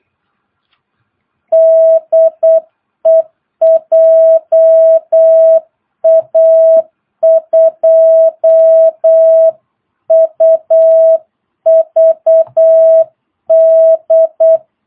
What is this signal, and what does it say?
Morse code, heard as a keyed tone, about 8 wpm, 660 Hz. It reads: DE JA2UVD
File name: jh8cbh / 2022cw